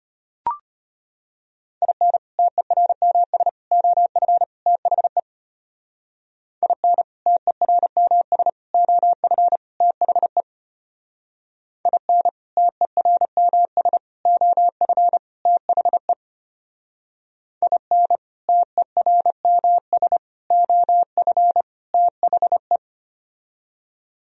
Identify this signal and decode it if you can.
Morse code, heard as a keyed tone, about 31 wpm, 700 Hz. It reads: IN TERMS OF THE IN TERMS OF THE IN TERMS OF THE IN TERMS OF THE
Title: Speed-Racing - Sets of 4 Words 25wpm